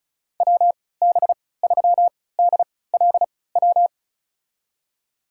Morse code, heard as a keyed tone, about 36 wpm, spaced as for 20 wpm, 700 Hz. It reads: WB3DLW
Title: USA Call Signs - 6 Characters - 3x Character Spacing - 35wpm